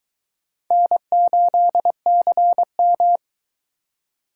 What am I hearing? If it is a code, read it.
Morse code, heard as a keyed tone, about 21 wpm, 700 Hz. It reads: N8CM